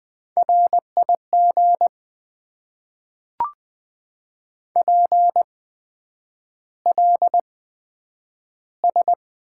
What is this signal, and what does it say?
Morse code, heard as a keyed tone, about 20 wpm, spaced as for 9 wpm, 700 Hz. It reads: RIG PLS
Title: Top 90 QSO Elements - 8x Character Spacing - 20wpm